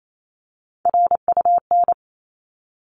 Morse code, heard as a keyed tone, about 28 wpm, 700 Hz. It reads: RUN
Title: Top 300 Words 28wpm